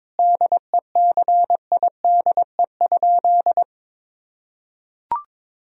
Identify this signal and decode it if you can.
Morse code, heard as a keyed tone, about 22 wpm, 700 Hz. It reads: DECIDE?
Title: Sentences from Top 1000 Words - Review Part 3 - 22wpm